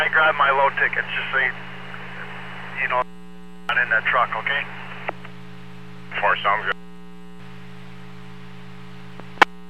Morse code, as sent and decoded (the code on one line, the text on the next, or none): none